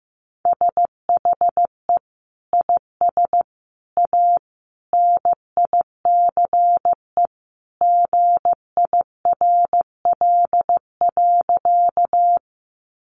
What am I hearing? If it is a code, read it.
Morse code, heard as a keyed tone, about 15 wpm, 700 Hz. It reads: SHE IS A NICE GIRL.
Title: Sentences from Top 1100 Words 15wpm